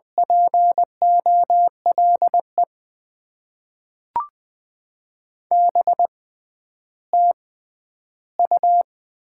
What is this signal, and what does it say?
Morse code, heard as a keyed tone, about 20 wpm, spaced as for 11 wpm, 700 Hz. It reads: POLE BTU
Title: Top 90 QSO Elements - 6x Character Spacing - 20wpm